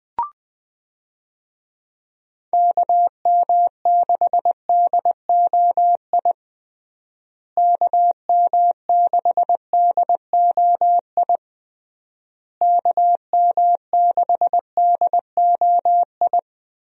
Morse code, heard as a keyed tone, about 19 wpm, 700 Hz. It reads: KM6DOI KM6DOI KM6DOI
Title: USA Call Signs - 6 Characters - Repeated 3x - 20wpm